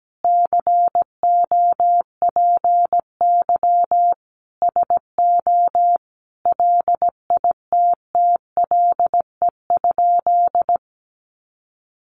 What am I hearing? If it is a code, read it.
Morse code, heard as a keyed tone, about 17 wpm, 700 Hz. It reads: COPY SO LITTLE?
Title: Sentences from Top 1400 Words 17wpm